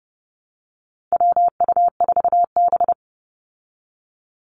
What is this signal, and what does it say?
Morse code, heard as a keyed tone, about 30 wpm, 700 Hz. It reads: WU4B
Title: Call Signs Repeated 2x - 30wpm